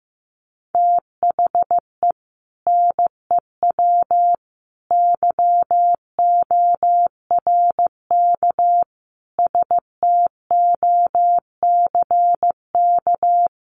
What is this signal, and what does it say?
Morse code, heard as a keyed tone, about 15 wpm, 700 Hz. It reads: THE NEW YORK STOCK
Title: Sets of 5 Words 15wpm